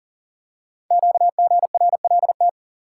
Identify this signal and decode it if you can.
Morse code, heard as a keyed tone, about 38 wpm, 700 Hz. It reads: QGRLT
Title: Code Groups - 5 Letters - 40wpm